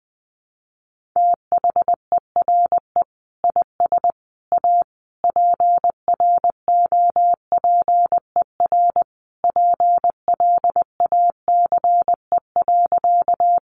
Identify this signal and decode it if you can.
Morse code, heard as a keyed tone, about 20 wpm, 700 Hz. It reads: THERE IS A PROPER PLACE.
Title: Sentences from Top 800 Words 20wpm